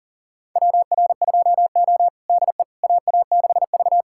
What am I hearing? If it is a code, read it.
Morse code, heard as a keyed tone, about 40 wpm, 700 Hz. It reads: WR2O DE AA6V
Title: USA Callsign DE Callsign - 4 Characters - 40wpm